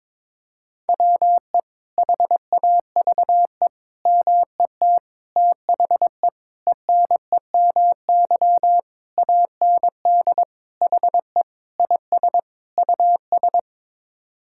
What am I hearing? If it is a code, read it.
Morse code, heard as a keyed tone, about 22 wpm, 700 Hz. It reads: WE HAVE MET THE ENEMY AND HE IS US
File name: Sets of 9 Words - 22wpm